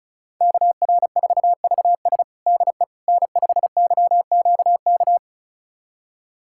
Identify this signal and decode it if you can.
Morse code, heard as a keyed tone, about 36 wpm, 700 Hz. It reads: KR4VS DE N5YQK
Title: USA Callsign DE Callsign - 5 Characters - 35wpm